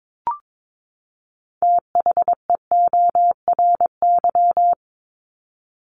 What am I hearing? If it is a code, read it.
Morse code, heard as a keyed tone, about 22 wpm, 700 Hz. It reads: THEORY
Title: Top 501-750 Words 22wpm